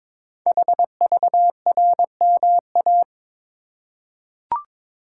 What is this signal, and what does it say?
Morse code, heard as a keyed tone, about 22 wpm, 700 Hz. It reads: HVRMA
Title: Code Groups - 5 Letters - 22wpm